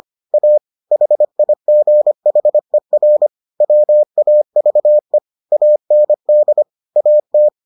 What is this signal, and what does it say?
Morse code, heard as a keyed tone, about 25 wpm, 600 Hz. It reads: A HIGHER WAVE AND AT